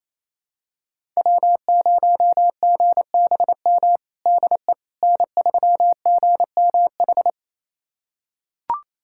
Morse code, heard as a keyed tone, about 28 wpm, 700 Hz. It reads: W0GBM DE N3GMH
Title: USA Callsign DE Callsign - 5 Characters - 28wpm